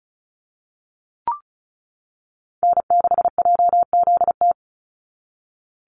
Morse code, heard as a keyed tone, about 34 wpm, 700 Hz. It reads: N6JZT
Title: NA SOTA Activator Callsigns 35wpm